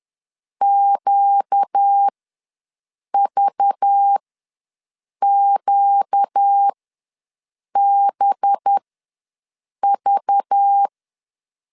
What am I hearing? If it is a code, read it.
Morse code, heard as a keyed tone, about 10 wpm, spaced as for 6 wpm, 800 Hz. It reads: QVQBV
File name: LEZIONI / 5 LBVC / 5 LBV ws 10wpm 3645